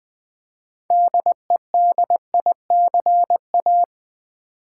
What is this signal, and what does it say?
Morse code, heard as a keyed tone, about 20 wpm, 700 Hz. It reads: DEDICA
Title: Build A Word - 3 Letters At A Time - 9-Letter Words - 20wpm